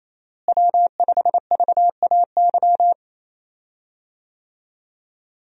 Morse code, heard as a keyed tone, about 28 wpm, 700 Hz. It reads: W5VAY
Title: USA Call Signs - 5 Characters - Repeated 2x - 28wpm